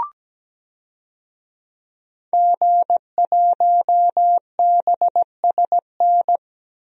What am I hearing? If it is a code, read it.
Morse code, heard as a keyed tone, about 17 wpm, 700 Hz. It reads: G1BSN
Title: DXCC-100 Call Signs - 5 Characters - 17wpm